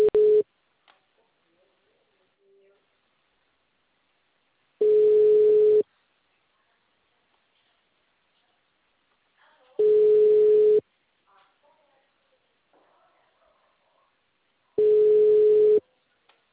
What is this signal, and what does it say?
Ringback tone (Europe, 425 Hz), with speech.